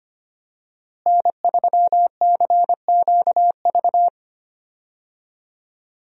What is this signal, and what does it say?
Morse code, heard as a keyed tone, about 25 wpm, 700 Hz. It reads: N3CQV
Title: USA Call Signs - 5 Characters - Repeated 2x - 25wpm